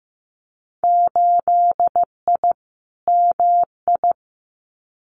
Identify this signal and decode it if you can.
Morse code, heard as a keyed tone, about 15 wpm, 700 Hz. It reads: 8I MI
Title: Winter Field Day Exchanges Encore 15wpm